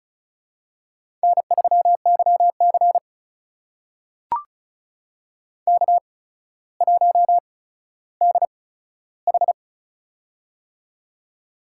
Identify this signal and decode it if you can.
Morse code, heard as a keyed tone, about 36 wpm, spaced as for 15 wpm, 700 Hz. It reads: N3YC K1DH
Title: USA Call Signs - 4 Characters - 8x Character Spacing - 35wpm